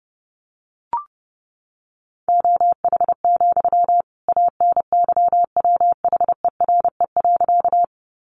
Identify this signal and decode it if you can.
Morse code, heard as a keyed tone, about 30 wpm, 700 Hz. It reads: OH, ANYWHERE.
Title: Sentences from Top 1400 Words 30wpm